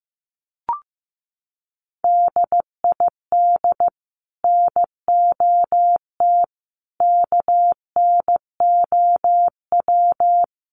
Morse code, heard as a keyed tone, about 15 wpm, 700 Hz. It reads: DID NOT KNOW
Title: Sets of 5 Words Encore2 15wpm